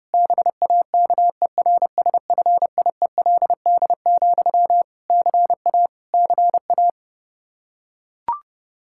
Morse code, heard as a keyed tone, about 32 wpm, 700 Hz. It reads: BAKERSFIELD, CA CA
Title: Practice QSO - Exchange 1 - Beginning - 30wpm